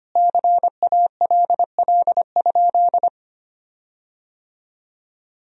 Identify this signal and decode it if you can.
Morse code, heard as a keyed tone, about 25 wpm, 700 Hz. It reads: CALL?
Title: 3x Word-Spacing - Sentences from Top 100 Words - 25wpm